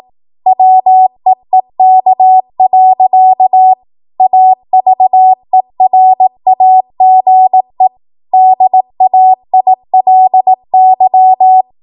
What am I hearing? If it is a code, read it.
Morse code, heard as a keyed tone, about 18 wpm, 750 Hz. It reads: WEEK. AVERAGE DAILY
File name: Archive / 18 WPM / 18WPM